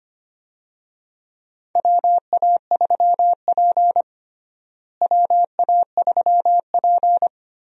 Morse code, heard as a keyed tone, about 25 wpm, 700 Hz. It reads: WA3P WA3P
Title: USA Call Signs - 4 Characters - Repeated 3x - 25wpm